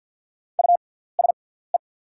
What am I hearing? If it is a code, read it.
Morse code, heard as a keyed tone, about 48 wpm, spaced as for 17 wpm, 700 Hz. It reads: USE
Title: 6x Character Spacing - Top 100 Words - 50wpm